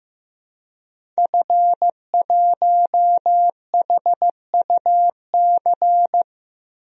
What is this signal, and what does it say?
Morse code, heard as a keyed tone, about 16 wpm, 700 Hz. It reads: F1HUC
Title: DXCC-100 Call Signs - 5 Characters - Repeated 3x - 15wpm